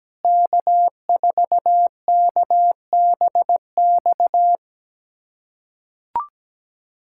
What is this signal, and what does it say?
Morse code, heard as a keyed tone, about 17 wpm, 700 Hz. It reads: K4KBX
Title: USA Callsign DE Callsign - 5 Characters - 17wpm